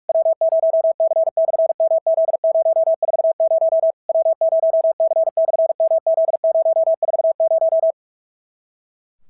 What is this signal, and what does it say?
Morse code, heard as a keyed tone, about 46 wpm, 650 Hz. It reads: W0C/MZ040 W0C/MZ040
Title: SOTA QSO Practice 45wpm